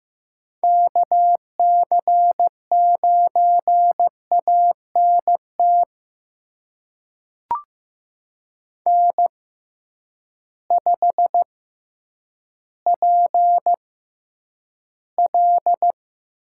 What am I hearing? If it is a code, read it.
Morse code, heard as a keyed tone, about 15 wpm, spaced as for 9 wpm, 700 Hz. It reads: KC9ANT N5PL
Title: USA Call Signs - 6x Character Spacing - 15wpm